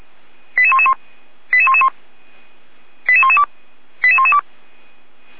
Five-tone selective call (ZVEI), heard as quick runs of five tones, tones of about 70 ms. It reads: 88111 88111 88112 88112
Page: SDR/ZVEI